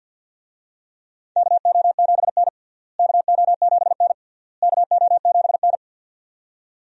Morse code, heard as a keyed tone, about 52 wpm, 700 Hz. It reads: KO7N KO7N KO7N
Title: USA Call Signs - 4 Characters - Repeated 3x - 50wpm